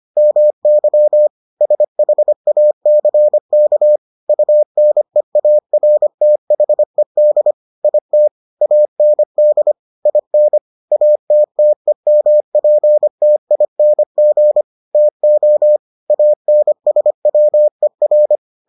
Morse code, heard as a keyed tone, about 25 wpm, 600 Hz. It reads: MY SHACK UNEARTHED IT AND IN ATTEMPTING TO ANSWER